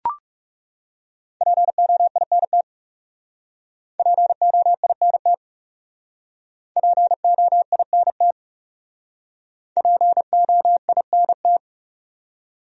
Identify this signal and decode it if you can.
Morse code, heard as a keyed tone, about 35 wpm, 700 Hz. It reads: POINT POINT POINT POINT
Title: Speed-Racing Top 200 Words 30wpm